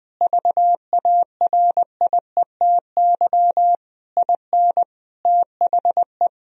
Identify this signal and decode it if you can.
Morse code, heard as a keyed tone, about 20 wpm, 700 Hz. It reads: VARIETY IN THE
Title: Sentences from Top 1000 Words - Review Part 4 - 20wpm